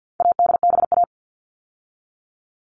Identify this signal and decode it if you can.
Morse code, heard as a keyed tone, about 50 wpm, 700 Hz. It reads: ADBA
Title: All Cut Numbers - 4-Digit 50wpm